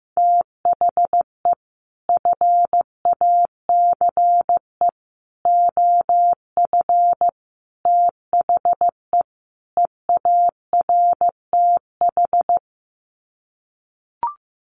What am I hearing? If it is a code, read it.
Morse code, heard as a keyed tone, about 15 wpm, 700 Hz. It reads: THE FACE OF THE EARTH
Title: Sets of 5 Words Encore 15wpm